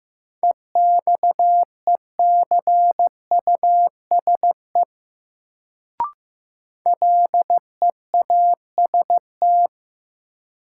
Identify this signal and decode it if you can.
Morse code, heard as a keyed tone, about 15 wpm, 700 Hz. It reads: EXECUSE LEAST
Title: Sentences from Top 700 Words - New Word Drill - 15wpm